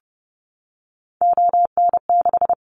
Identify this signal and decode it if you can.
Morse code, heard as a keyed tone, about 30 wpm, 700 Hz. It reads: ON6
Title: Callsign Prefixes for Top 100 DXCC - 30wpm